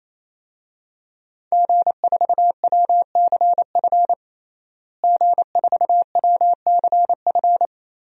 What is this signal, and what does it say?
Morse code, heard as a keyed tone, about 28 wpm, 700 Hz. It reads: G4WCF G4WCF
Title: DXCC-100 Call Signs - 5 Characters - Repeated 3x - 28wpm